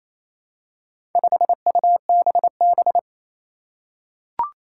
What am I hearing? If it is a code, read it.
Morse code, heard as a keyed tone, about 27 wpm, 700 Hz. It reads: 5UBB
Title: Four Challenging Letter-Number Combinations 28wpm